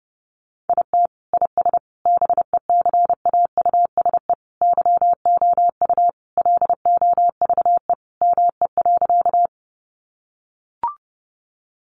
Morse code, heard as a keyed tone, about 30 wpm, 700 Hz. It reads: IT IS BECAUSE YOU LOVE ME.
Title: Sentences from Top 600 Words 30wpm